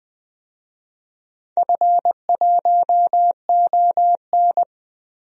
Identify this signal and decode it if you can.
Morse code, heard as a keyed tone, about 20 wpm, 700 Hz. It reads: F1ON
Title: DXCC-100 Call Signs - 4 Characters - Repeated 3x - 20wpm